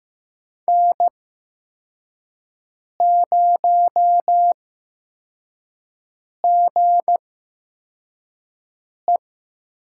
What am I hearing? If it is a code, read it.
Morse code, heard as a keyed tone, about 15 wpm, spaced as for 4 wpm, 700 Hz. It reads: N0GE
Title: USA Call Signs - 8x Character Spacing - 15wpm